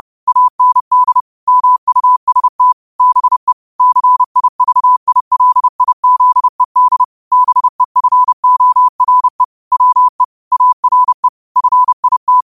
Morse code, heard as a keyed tone, about 30 wpm, 1000 Hz. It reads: AND MUST BE CIVILIZED BEFORE WE ARE FIT